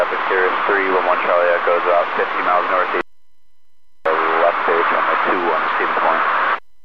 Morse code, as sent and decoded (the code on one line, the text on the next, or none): none